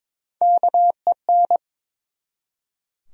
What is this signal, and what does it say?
Morse code, heard as a keyed tone, about 22 wpm, 700 Hz. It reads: KEN
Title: USA Common Names - Max 3 Letters - 22wpm